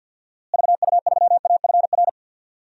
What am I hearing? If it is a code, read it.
Morse code, heard as a keyed tone, about 52 wpm, 700 Hz. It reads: VU3AFR
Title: DXCC-100 Call Signs - 6 Characters - Repeated 3x - 50wpm